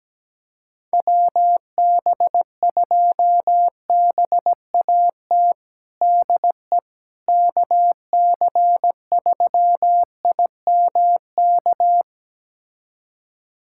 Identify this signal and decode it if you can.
Morse code, heard as a keyed tone, about 17 wpm, 700 Hz. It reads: WB2BAT DE KC3IMK
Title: USA Callsign DE Callsign - 6 Characters - 17wpm